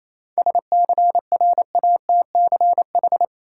Morse code, heard as a keyed tone, about 28 wpm, 700 Hz. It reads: SCRATCH
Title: Seven Letter Words - 28wpm